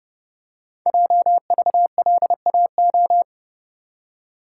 Morse code, heard as a keyed tone, about 30 wpm, 700 Hz. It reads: JVLAO
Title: Code Groups - 5 Letter:Numbers - 30wpm